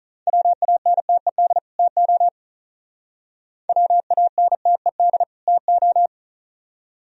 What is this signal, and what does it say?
Morse code, heard as a keyed tone, about 42 wpm, 700 Hz. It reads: WANTED TO WANTED TO
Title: Speed-Racing - Sets of 2 Words 35wpm